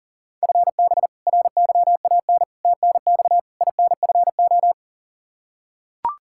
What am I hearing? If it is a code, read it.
Morse code, heard as a keyed tone, about 40 wpm, 700 Hz. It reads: FB RYAN TNX INFO
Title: K1USN SST QSO Practice 40wpm